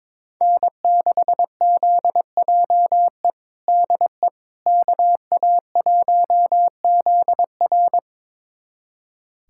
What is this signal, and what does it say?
Morse code, heard as a keyed tone, about 22 wpm, 700 Hz. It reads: N6ZJE DE KA1ZR
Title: USA Callsign DE Callsign - 5 Characters - 22wpm